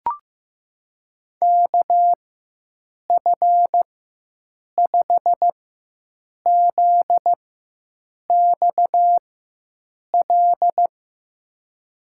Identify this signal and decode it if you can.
Morse code, heard as a keyed tone, about 16 wpm, spaced as for 7 wpm, 700 Hz. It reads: KF5ZXL
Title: USA Call Signs - 6 Characters - 4x Character Spacing - 15wpm